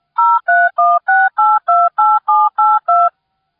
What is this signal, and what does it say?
DTMF, heard as telephone touch tones, tones of about 220 ms. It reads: 0316828782